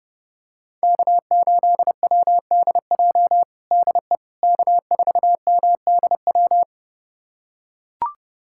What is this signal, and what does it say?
Morse code, heard as a keyed tone, about 32 wpm, 700 Hz. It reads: K8WDJ DE K4MDW
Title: USA Callsign DE Callsign - 5 Characters - 30wpm